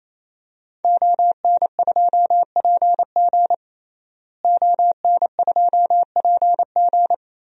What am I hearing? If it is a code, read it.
Morse code, heard as a keyed tone, about 28 wpm, 700 Hz. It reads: ON2PG ON2PG
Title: DXCC-100 Call Signs - 5 Characters - Repeated 3x - 28wpm